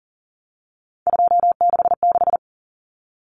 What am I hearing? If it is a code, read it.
Morse code, heard as a keyed tone, about 40 wpm, 700 Hz. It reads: 266
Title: Common Cut Numbers - 3-Digit 40wpm